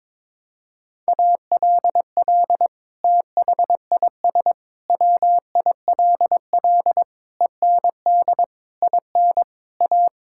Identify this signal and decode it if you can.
Morse code, heard as a keyed tone, about 22 wpm, 700 Hz. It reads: ALL THIS WILL END IN A
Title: Sentences from Top 1000 Words - Review Part 5 - 22wpm